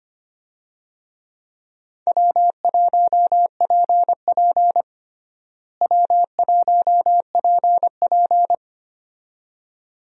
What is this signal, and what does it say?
Morse code, heard as a keyed tone, about 25 wpm, 700 Hz. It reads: W1PP W1PP
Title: USA Call Signs - 4 Characters - Repeated 2x - 25wpm